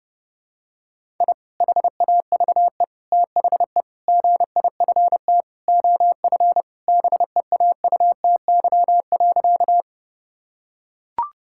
Morse code, heard as a keyed tone, about 30 wpm, 700 Hz. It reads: I HAVE THE GIFT OF BEAUTY.
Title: Sentences from Top 1000 Words - Review Part 6 - 30wpm